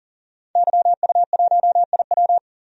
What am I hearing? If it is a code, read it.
Morse code, heard as a keyed tone, about 37 wpm, 700 Hz. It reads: YU1IW
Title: DXCC-100 Call Signs - 5 Characters - 40wpm